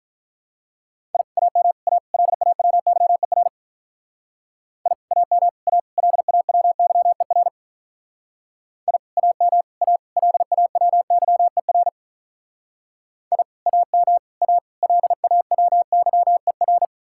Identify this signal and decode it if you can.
Morse code, heard as a keyed tone, about 46 wpm, 700 Hz. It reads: I AM A LAWYER I AM A LAWYER I AM A LAWYER I AM A LAWYER